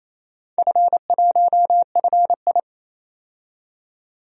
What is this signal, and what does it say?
Morse code, heard as a keyed tone, about 28 wpm, 700 Hz. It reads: F1FI
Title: DXCC-100 Call Signs - 4 Characters - 28wpm